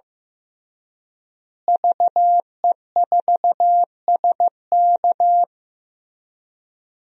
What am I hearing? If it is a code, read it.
Morse code, heard as a keyed tone, about 15 wpm, 700 Hz. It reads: VE4SK
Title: DXCC-100 Call Signs - 5 Characters - Repeated 2x - 15wpm